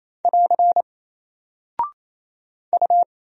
Morse code, heard as a keyed tone, about 27 wpm, 700 Hz. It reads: <AR> U
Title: Single Character (Letter:Number:Punctuation:Prosign) 28wpm